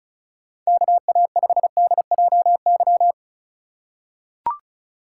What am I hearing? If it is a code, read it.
Morse code, heard as a keyed tone, about 36 wpm, 700 Hz. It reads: KA5DJY
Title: USA Call Signs - 6 Characters - 2x Character Spacing - 35wpm